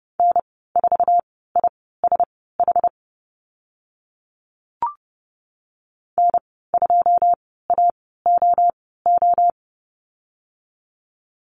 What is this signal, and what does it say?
Morse code, heard as a keyed tone, about 30 wpm, spaced as for 17 wpm, 700 Hz. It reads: N4ISH N2AOO
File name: Lesson 15g - Number 5 - Call Signs 30wpm